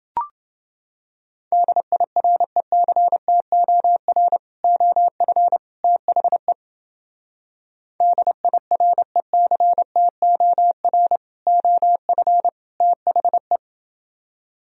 Morse code, heard as a keyed tone, about 30 wpm, 700 Hz. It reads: DIRECTOR OF THE DIRECTOR OF THE